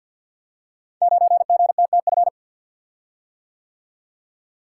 Morse code, heard as a keyed tone, about 50 wpm, 700 Hz. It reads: 9GTTF
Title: Code Groups - 5 Letter:Numbers - 50wpm